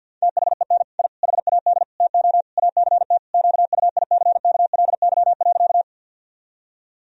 Morse code, heard as a keyed tone, about 51 wpm, 700 Hz. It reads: THEN I HAD TO ACT QUICKLY.
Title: Sentences from Top 1000 Words - Review Part 1 - 50wpm